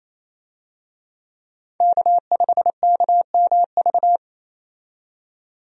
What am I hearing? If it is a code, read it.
Morse code, heard as a keyed tone, about 28 wpm, 700 Hz. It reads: K5KMV
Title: USA Call Signs - 5 Characters - 28wpm